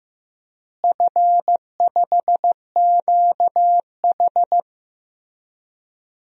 Morse code, heard as a keyed tone, about 15 wpm, 700 Hz. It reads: F5QH
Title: DXCC-100 Call Signs - 4 Characters - Repeated 3x - 15wpm